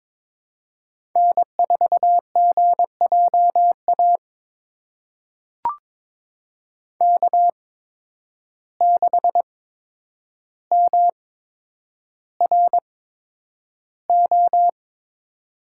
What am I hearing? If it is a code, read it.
Morse code, heard as a keyed tone, about 22 wpm, spaced as for 9 wpm, 700 Hz. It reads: N4GJA K6MRO
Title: USA Call Signs - 5 Characters - 8x Character Spacing - 22wpm